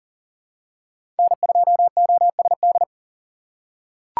Morse code, heard as a keyed tone, about 40 wpm, 700 Hz. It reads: N2OSD
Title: USA Call Signs - 5 Characters - 6x Character Spacing - 40wpm